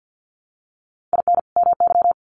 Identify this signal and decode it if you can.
Morse code, heard as a keyed tone, about 50 wpm, 700 Hz. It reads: IN MY
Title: Sets of 2 Words 50wpm